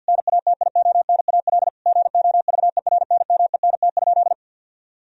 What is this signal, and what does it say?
Morse code, heard as a keyed tone, about 50 wpm, 700 Hz. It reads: NATIONAL GOVERNMENT?
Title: Sentences from Top 1000 Words - Review Part 4 - 50wpm